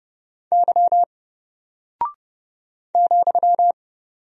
Morse code, heard as a keyed tone, about 30 wpm, 700 Hz. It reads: Y ,
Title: Single Character (Letter:Number:Punctuation:Prosign) Alt 30wpm